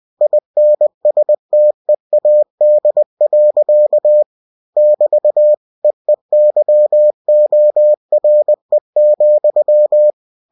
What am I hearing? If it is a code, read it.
Morse code, heard as a keyed tone, about 20 wpm, 600 Hz. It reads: INSTEAD. <BT> EEYORE,